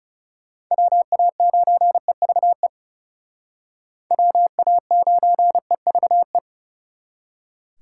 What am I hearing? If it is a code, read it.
Morse code, heard as a keyed tone, about 37 wpm, 700 Hz. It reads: WA9EVE WA9EVE
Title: Speed Racing - USA 2x3 Call Signs 30wpm